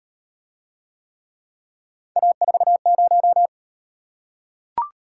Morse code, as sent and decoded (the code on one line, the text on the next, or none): .- ....- -----
A40